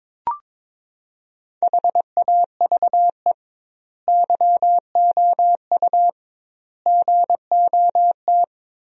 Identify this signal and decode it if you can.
Morse code, heard as a keyed tone, about 23 wpm, 700 Hz. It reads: HAVE YOU GOT
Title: 2x Word-Spacing - Sentences from Top 300 Words - 22wpm